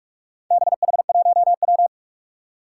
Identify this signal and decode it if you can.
Morse code, heard as a keyed tone, about 46 wpm, 700 Hz. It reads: BH1W
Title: Four Challenging Letter-Number Combinations 45wpm